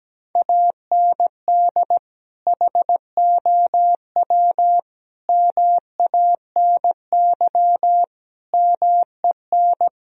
Morse code, heard as a keyed tone, about 17 wpm, 700 Hz. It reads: AND HOW MANY MEN
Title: Sentences from Top 1000 Words - Review Part 1 - 17wpm